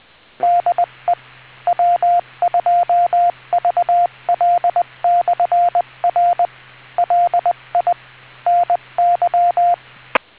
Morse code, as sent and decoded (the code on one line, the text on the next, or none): -.. / .-- ..--- ...- .-.. -..-. .-. / .-.. .. / -. -.--
DE W2VL/R LI NY